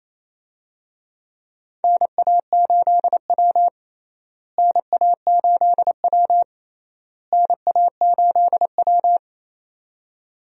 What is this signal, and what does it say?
Morse code, heard as a keyed tone, about 27 wpm, 700 Hz. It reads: NA8W NA8W NA8W
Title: USA Call Signs - 4 Characters - Repeated 3x - 28wpm